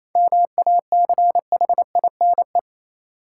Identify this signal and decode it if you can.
Morse code, heard as a keyed tone, about 28 wpm, 700 Hz. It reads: MACHINE
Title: Sentences from Top 1000 Words - Review Part 2 - New Word Drill - 28wpm